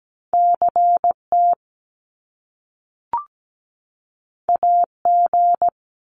Morse code, heard as a keyed tone, about 17 wpm, 700 Hz. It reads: CT AG
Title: Top Two Letter Combinations in Words 17wpm